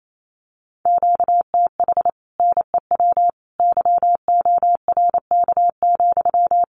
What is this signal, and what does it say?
Morse code, heard as a keyed tone, about 28 wpm, 700 Hz. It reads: QTH NEW YORK,
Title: Common QSO Phrases Encore 28wpm